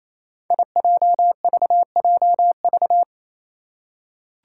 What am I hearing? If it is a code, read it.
Morse code, heard as a keyed tone, about 28 wpm, 700 Hz. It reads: IJVJV
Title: Five Challenging Letter-Number Combinations 28wpm